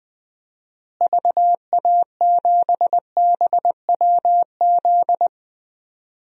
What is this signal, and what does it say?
Morse code, heard as a keyed tone, about 20 wpm, 700 Hz. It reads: VA7BWZ
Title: DXCC-100 Call Signs - 6 Characters - Repeated 2x - 20wpm